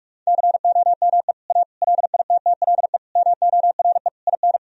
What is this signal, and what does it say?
Morse code, heard as a keyed tone, about 46 wpm, 700 Hz. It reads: COME A LITTLE MORE IN
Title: Sentences from Top 1000 Words - Review Part 2 - 45wpm